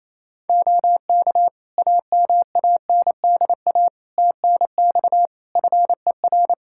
Morse code, heard as a keyed tone, about 28 wpm, 700 Hz. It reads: OK AMANDA TNX FER
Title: K1USN SST QSO Practice 28wpm